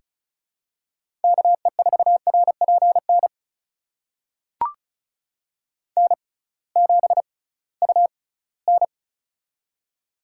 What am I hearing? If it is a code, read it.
Morse code, heard as a keyed tone, about 36 wpm, spaced as for 21 wpm, 700 Hz. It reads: KE4RPN N7UN
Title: USA Call Signs - 6x Character Spacing - 35wpm